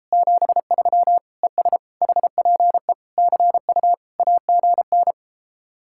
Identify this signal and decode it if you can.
Morse code, heard as a keyed tone, about 33 wpm, 700 Hz. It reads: 73 ES HPE CU AGN